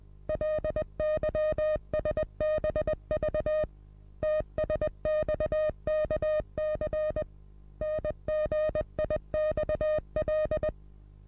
Morse code, heard as a keyed tone, about 20 wpm, 620 Hz. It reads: LYSBV TSXKC NGIXL
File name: repos / Morsekurs / DJ4UF Morsen27